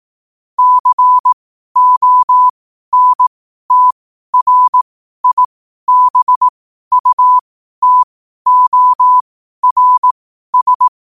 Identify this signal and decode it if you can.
Morse code, heard as a keyed tone, about 18 wpm, spaced as for 12 wpm, 1000 Hz. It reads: CONTRIBUTORS
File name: qotdfiles / 12wpm